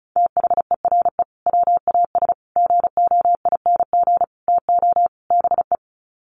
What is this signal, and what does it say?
Morse code, heard as a keyed tone, about 34 wpm, 700 Hz. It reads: THERE WAS GOING TO BE
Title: Sets of 5 Words Encore2 35wpm